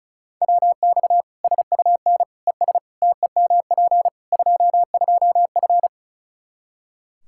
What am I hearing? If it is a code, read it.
Morse code, heard as a keyed tone, about 36 wpm, 700 Hz. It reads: WX SUN ES TEMP 22F
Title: Practice QSO - Exchange 2 - Info - 35wpm